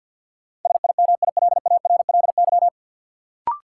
Morse code, heard as a keyed tone, about 52 wpm, 700 Hz. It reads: SIMILARLY
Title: Sentences from Top 1400 Words - New Word Drill - 50wpm